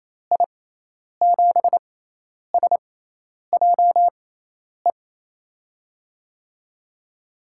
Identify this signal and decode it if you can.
Morse code, heard as a keyed tone, about 28 wpm, spaced as for 10 wpm, 700 Hz. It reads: I7SJE
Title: USA Call Signs - 6 Characters - 6x Character Spacing - 28wpm